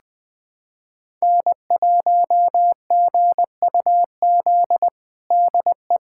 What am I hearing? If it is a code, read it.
Morse code, heard as a keyed tone, about 20 wpm, 690 Hz. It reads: N1GUZ DE
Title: USA Callsign DE Callsign 20wpm